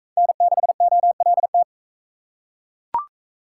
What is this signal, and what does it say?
Morse code, heard as a keyed tone, about 40 wpm, 700 Hz. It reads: N6OLT